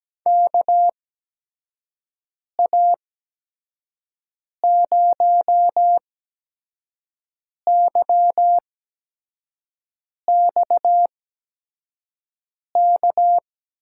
Morse code, heard as a keyed tone, about 17 wpm, spaced as for 5 wpm, 700 Hz. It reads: KA0YXK
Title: USA Call Signs - 8x Character Spacing - 17wpm